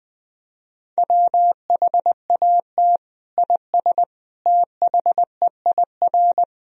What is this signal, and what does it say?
Morse code, heard as a keyed tone, about 19 wpm, 700 Hz. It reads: WHAT IS THEIR
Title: Sentences from Top 1000 Words - Review Part 1 - 20wpm